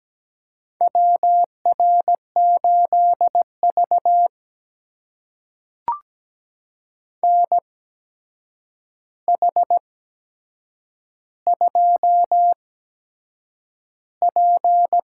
Morse code, heard as a keyed tone, about 17 wpm, spaced as for 7 wpm, 700 Hz. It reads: WR8V NH2P
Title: USA Call Signs - 4 Characters - 8x Character Spacing - 17wpm